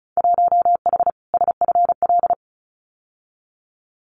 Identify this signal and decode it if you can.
Morse code, heard as a keyed tone, about 34 wpm, 700 Hz. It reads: 1H SFL
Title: Winter Field Day Exchanges 35wpm